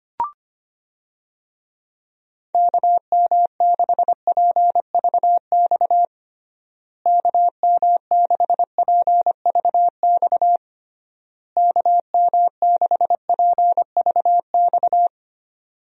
Morse code, heard as a keyed tone, about 25 wpm, 700 Hz. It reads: KM6PVX KM6PVX KM6PVX
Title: USA Call Signs - 6 Characters - Repeated 3x - 25wpm